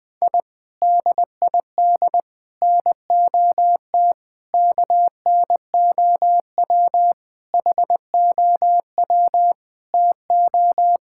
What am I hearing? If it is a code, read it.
Morse code, heard as a keyed tone, about 21 wpm, 700 Hz. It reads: I DID NOT KNOW HOW TO
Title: Sentences from Top 1000 Words - Review Part 6 - 20wpm